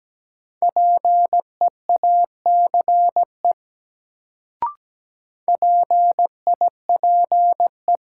Morse code, heard as a keyed tone, about 17 wpm, 700 Hz. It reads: PEACE PIPE
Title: Sentences from Top 1500 Words - New Word Drill Alt - 17wpm